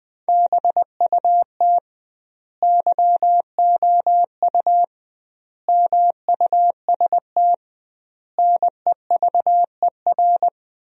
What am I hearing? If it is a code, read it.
Morse code, heard as a keyed tone, about 20 wpm, 700 Hz. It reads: BUT YOU MUST NEVER
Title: 2x Word-Spacing - Sentences from Top 300 Words - 20wpm